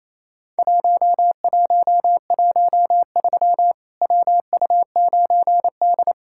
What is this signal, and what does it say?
Morse code, heard as a keyed tone, about 28 wpm, 700 Hz. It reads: 1113 WU9D
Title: Worked All Europe DX Contest - QTC Exchange - 28wpm